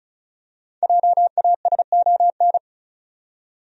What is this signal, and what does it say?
Morse code, heard as a keyed tone, about 34 wpm, 700 Hz. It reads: JASON
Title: USA Common Names - Max 5 Letters - 35wpm